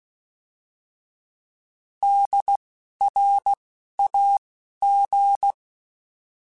Morse code, heard as a keyed tone, about 16 wpm, 800 Hz. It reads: D R A G
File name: Alphabetical / Alphabetical mp3s / Ds @10